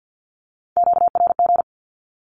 Morse code, heard as a keyed tone, about 50 wpm, 700 Hz. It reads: QRG
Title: Q-Codes with Spoken Meaning - 50wpm